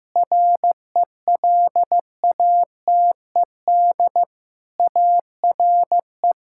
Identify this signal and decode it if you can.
Morse code, heard as a keyed tone, about 16 wpm, 700 Hz. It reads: RELATED ARE
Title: Sentences from Top 1000 Words - Review Part 5 - 15wpm